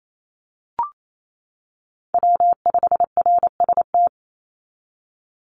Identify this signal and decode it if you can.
Morse code, heard as a keyed tone, about 28 wpm, 700 Hz. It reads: W5RST
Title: NA SOTA Chaser Callsigns 28wpm